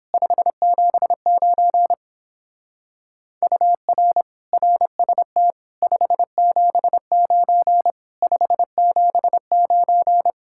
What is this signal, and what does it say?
Morse code, heard as a keyed tone, about 28 wpm, 700 Hz. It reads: 579 UR RST 579 579